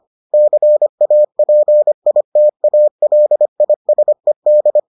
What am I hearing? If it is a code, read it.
Morse code, heard as a keyed tone, about 25 wpm, 600 Hz. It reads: CAPITALISED